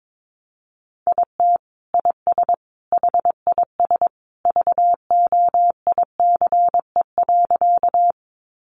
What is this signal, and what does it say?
Morse code, heard as a keyed tone, about 22 wpm, 700 Hz. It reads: IT IS HIS VOICE.
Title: Sentences from Top 500 Words 22wpm